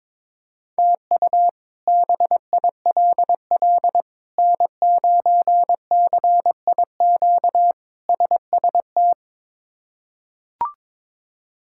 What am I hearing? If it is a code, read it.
Morse code, heard as a keyed tone, about 22 wpm, 700 Hz. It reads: TU BILL N9CIQ SST
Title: K1USN SST QSO Practice - 22wpm